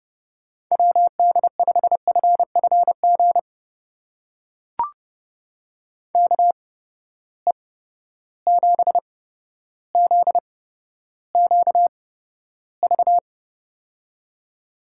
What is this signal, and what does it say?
Morse code, heard as a keyed tone, about 30 wpm, spaced as for 13 wpm, 700 Hz. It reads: WD5FFG KE7ZQV
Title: USA Call Signs - 6 Characters - 8x Character Spacing - 30wpm